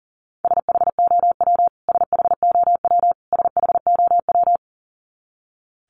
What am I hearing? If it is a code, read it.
Morse code, heard as a keyed tone, about 37 wpm, 700 Hz. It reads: SHOW SHOW SHOW
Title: Top 500 Words Repeated 40wpm